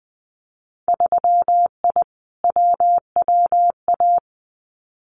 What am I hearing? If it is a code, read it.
Morse code, heard as a keyed tone, about 20 wpm, 700 Hz. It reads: 3I WWA
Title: Winter Field Day Exchanges Encore 20wpm